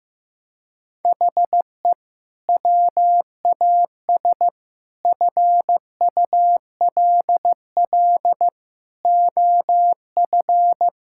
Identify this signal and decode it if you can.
Morse code, heard as a keyed tone, about 15 wpm, 700 Hz. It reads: HE WAS FULL OF
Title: Sentences from Top 1000 Words - Review Part 6 - 15wpm